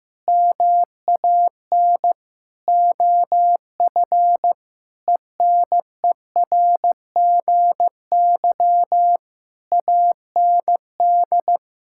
Morse code, heard as a keyed tone, about 15 wpm, 700 Hz. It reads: MAN OF ENERGY AND
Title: Sentences from Top 1000 Words - Review Part 6 - 15wpm